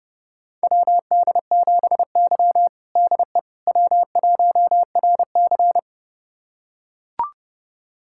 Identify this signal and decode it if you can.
Morse code, heard as a keyed tone, about 29 wpm, 700 Hz. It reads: WD7Y DE W1RC
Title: USA Callsign DE Callsign - 4 Characters - 30wpm